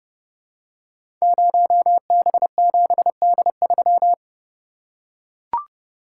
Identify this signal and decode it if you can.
Morse code, heard as a keyed tone, about 30 wpm, 700 Hz. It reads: 0B7D3
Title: Code Groups - 5 Letter:Numbers - 30wpm